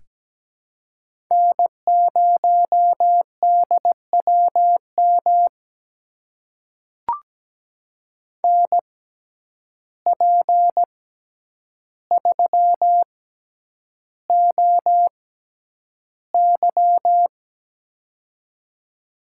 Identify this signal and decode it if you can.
Morse code, heard as a keyed tone, about 17 wpm, spaced as for 9 wpm, 700 Hz. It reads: N0DWM NP3OY
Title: USA Call Signs - 5 Characters - 6x Character Spacing - 17wpm